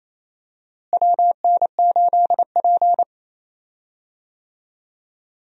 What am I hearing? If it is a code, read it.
Morse code, heard as a keyed tone, about 28 wpm, 700 Hz. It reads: WN8P
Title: USA Call Signs - 4 Characters - Repeated 3x - 28wpm